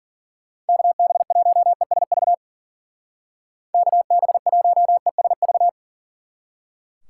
Morse code, heard as a keyed tone, about 50 wpm, 700 Hz. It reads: KB1ESV KB1ESV
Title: Speed Racing - USA 2x3 Call Signs 40wpm